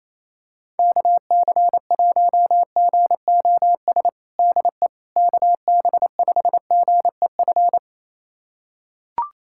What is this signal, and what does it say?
Morse code, heard as a keyed tone, about 28 wpm, 700 Hz. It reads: KC1GOS DE KB5GEF
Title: USA Callsign DE Callsign - 6 Characters - 28wpm